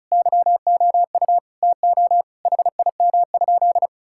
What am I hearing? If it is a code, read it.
Morse code, heard as a keyed tone, about 36 wpm, 700 Hz. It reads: YOU TO HIM?
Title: Sentences from Top 1600 Words 35wpm